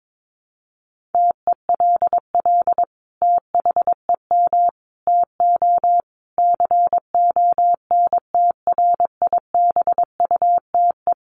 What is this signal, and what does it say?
Morse code, heard as a keyed tone, about 22 wpm, 700 Hz. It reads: TELL THEM TO CONTRIBUTE
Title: Sentences from Top 1400 Words 22wpm